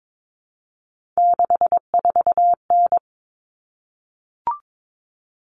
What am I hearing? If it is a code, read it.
Morse code, heard as a keyed tone, about 22 wpm, 700 Hz. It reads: 64N
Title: All Cut Numbers - 3-Digit 22wpm